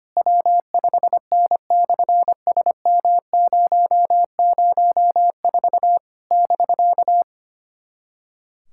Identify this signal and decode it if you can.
Morse code, heard as a keyed tone, about 25 wpm, 700 Hz. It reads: W5N/SM004 <BK>
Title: SOTA QSO Practice 25wpm